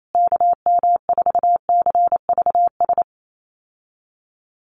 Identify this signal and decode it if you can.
Morse code, heard as a keyed tone, about 28 wpm, 690 Hz. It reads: KM4CVS
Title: USA Call Signs Repeated 3x 28wpm